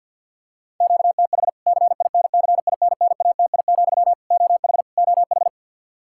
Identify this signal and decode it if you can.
Morse code, heard as a keyed tone, about 48 wpm, 700 Hz. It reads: QTH CINCINNATI, OH OH
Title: K1USN SST QSO Practice 50wpm